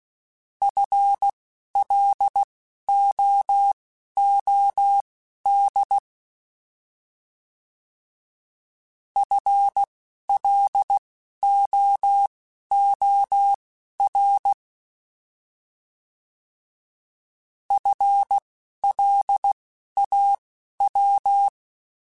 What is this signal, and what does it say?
Morse code, heard as a keyed tone, about 16 wpm, spaced as for 11 wpm, 800 Hz. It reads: FLOOD FLOOR FLAW
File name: Alphabetical / Alphabetical mp3s / Fs @10